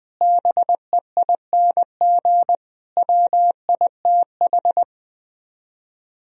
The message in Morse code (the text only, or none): BEING WITH